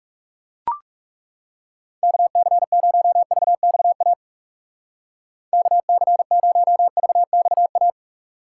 Morse code, heard as a keyed tone, about 37 wpm, 700 Hz. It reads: KC0VXA KC0VXA
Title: Speed Racing - USA 2x3 Call Signs 30wpm